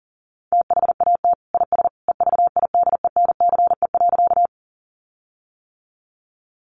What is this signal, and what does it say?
Morse code, heard as a keyed tone, about 40 wpm, 700 Hz. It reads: THAT IS EVIDENCE.
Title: Sentences from Top 1300 Words 40wpm